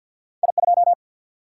Morse code, heard as a keyed tone, about 53 wpm, 700 Hz. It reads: I2
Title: Single-Letter Single-Number 50wpm